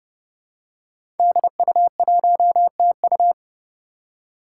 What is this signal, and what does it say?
Morse code, heard as a keyed tone, about 28 wpm, 700 Hz. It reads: DU1TU
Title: Five Challenging Letter-Number Combinations 30wpm